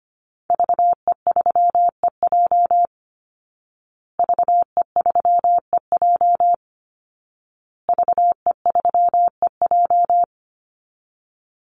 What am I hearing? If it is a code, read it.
Morse code, heard as a keyed tone, about 25 wpm, 700 Hz. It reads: VE3EJ VE3EJ VE3EJ
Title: Call Signs Repeated 2x - 25wpm